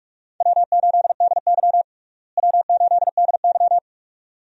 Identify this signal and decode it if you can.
Morse code, heard as a keyed tone, about 48 wpm, 700 Hz. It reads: W8DY W8DY
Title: USA Call Signs - 4 Characters - Repeated 2x - 45wpm